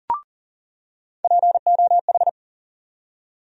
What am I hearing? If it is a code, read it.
Morse code, heard as a keyed tone, about 38 wpm, 700 Hz. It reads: POH
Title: USA Call Sign Postfixes - 3 Letters 40wpm